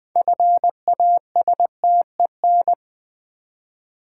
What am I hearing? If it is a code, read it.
Morse code, heard as a keyed tone, about 20 wpm, 700 Hz. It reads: FASTEN
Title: Build A Word - 2 Letters At A Time - 6-Letter Words - 20wpm